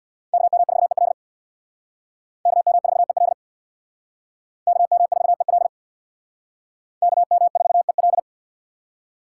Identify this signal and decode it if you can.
Morse code, heard as a keyed tone, about 62 wpm, 700 Hz. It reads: KM4EL KM4EL KM4EL KM4EL
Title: Speed Racing - USA 2x2 Call Signs 50wpm